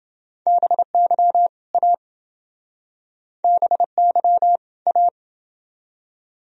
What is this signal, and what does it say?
Morse code, heard as a keyed tone, about 28 wpm, 700 Hz. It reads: BY A BY A